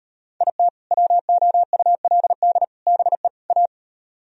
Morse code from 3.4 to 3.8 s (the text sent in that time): A